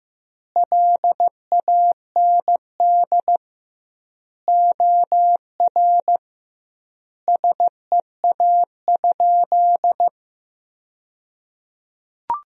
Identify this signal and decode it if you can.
Morse code, heard as a keyed tone, about 16 wpm, 700 Hz. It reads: LAND OR SEA?
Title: 2x Word-Spacing - Sentences from Top 500 Words - 15wpm